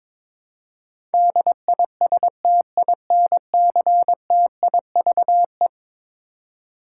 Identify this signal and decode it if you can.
Morse code, heard as a keyed tone, about 22 wpm, 700 Hz. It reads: DISTINCTIVE
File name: practice / 22wpm / Eleven Letter Words - 22wpm